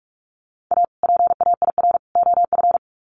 Morse code, heard as a keyed tone, about 44 wpm, 700 Hz. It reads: A PAIR OF
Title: Sets of 3 Words 45wpm